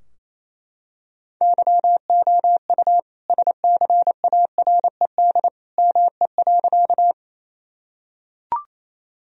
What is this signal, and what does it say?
Morse code, heard as a keyed tone, about 27 wpm, 700 Hz. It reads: YOU SCARED ME.
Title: Sentences from Top 1000 Words - Review Part 4 - 28wpm